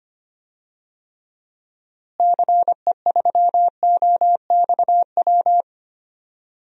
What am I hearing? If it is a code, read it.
Morse code, heard as a keyed tone, about 25 wpm, 700 Hz. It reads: CE3OXW
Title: DXCC-100 Call Signs - 6 Characters - Repeated 3x - 25wpm